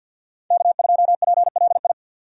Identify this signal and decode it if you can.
Morse code, heard as a keyed tone, about 53 wpm, 700 Hz. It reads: K2PLI
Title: USA Call Signs - 5 Characters - 3x Character Spacing - 50wpm